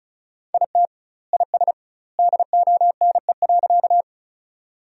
Morse code, heard as a keyed tone, about 36 wpm, 700 Hz. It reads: IT IS DONE.
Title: 2x Word-Spacing - Sentences from Top 500 Words - 35wpm